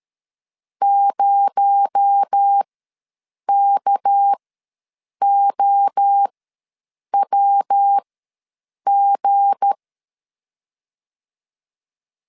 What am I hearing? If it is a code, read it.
Morse code, heard as a keyed tone, about 13 wpm, spaced as for 7 wpm, 800 Hz. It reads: 0KOWG
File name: LEZIONI / 7 GF9Z / 7 GF9 ws 12wpm 3344